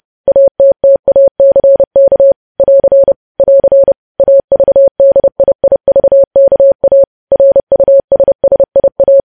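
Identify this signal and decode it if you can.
Morse code, heard as a keyed tone, about 30 wpm, 550 Hz. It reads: ATTACK <AR> <AR> AVDIIVKA RUSSIA
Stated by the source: News headlines sent in Morse code at 30 wpm.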